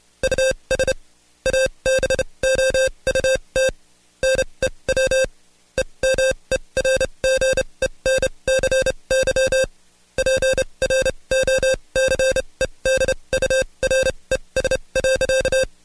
Morse code, heard as a keyed tone, about 29 wpm, spaced as for 23 wpm, 530 Hz. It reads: US ABOUT NEW EMERGENCY PROCEDURES.
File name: Code Quick / AudioDownld2 / Seg138-Stepper11 23WPM